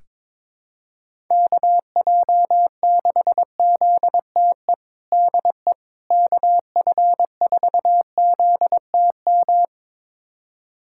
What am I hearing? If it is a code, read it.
Morse code, heard as a keyed tone, about 21 wpm, 700 Hz. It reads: KJ6ZTE DE KF4ZTM
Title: USA Callsign DE Callsign - 6 Characters - 22wpm